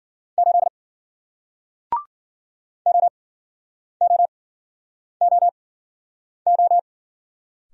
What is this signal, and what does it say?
Morse code, heard as a keyed tone, about 54 wpm, spaced as for 9 wpm, 700 Hz. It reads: 8 OOOO